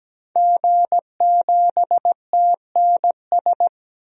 Morse code, heard as a keyed tone, about 17 wpm, 700 Hz. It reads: G7TNS
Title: DXCC-100 Call Signs - 5 Characters - 17wpm